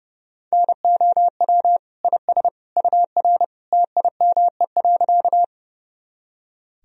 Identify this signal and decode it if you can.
Morse code, heard as a keyed tone, about 32 wpm, 700 Hz. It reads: NOW IS UR TIME.
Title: Sentences from Top 100 Words and Abbreviations 30wpm